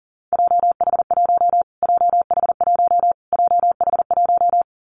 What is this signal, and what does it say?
Morse code, heard as a keyed tone, about 40 wpm, 700 Hz. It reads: JH1 JH1 JH1
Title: Callsign Prefixes for Top 25 DXCC - Repeated 3x - 40wpm